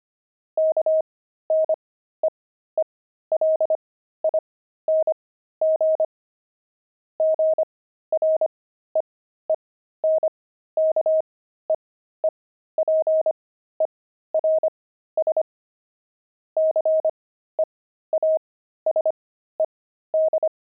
Morse code, heard as a keyed tone, about 25 wpm, spaced as for 13 wpm, 630 Hz. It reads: KNEELING GREENKEEPERS CEASED
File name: pangram-fs-13-cs-25-beehive